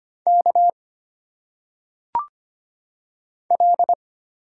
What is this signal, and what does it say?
Morse code, heard as a keyed tone, about 25 wpm, 700 Hz. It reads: K L
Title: Single Letters - Phonetic Pronunciation 25wpm